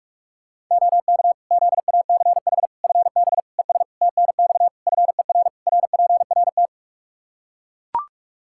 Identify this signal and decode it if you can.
Morse code, heard as a keyed tone, about 41 wpm, 700 Hz. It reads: OK ZACH FB ES TNX FER RPRT
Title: Practice QSO - Exchange 2 - Info - 45wpm